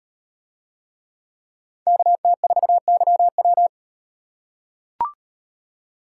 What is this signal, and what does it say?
Morse code, heard as a keyed tone, about 36 wpm, 700 Hz. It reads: KT4YW